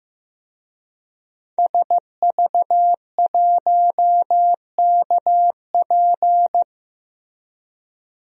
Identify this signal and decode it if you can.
Morse code, heard as a keyed tone, about 15 wpm, 700 Hz. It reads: SV1KP
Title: DXCC-100 Call Signs - 5 Characters - Repeated 2x - 15wpm